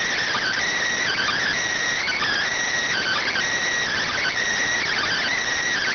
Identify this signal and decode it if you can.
Emergency Alert System (SAME) tones (header bursts), heard as harsh buzzing bursts. It is heard from 1.0 to 5.4 s.